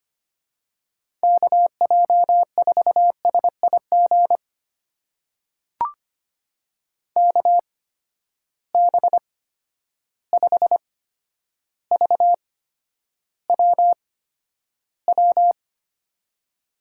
Morse code, heard as a keyed tone, about 25 wpm, spaced as for 11 wpm, 700 Hz. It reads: KJ4SIG KB5VWW
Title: USA Call Signs - 6 Characters - 8x Character Spacing - 25wpm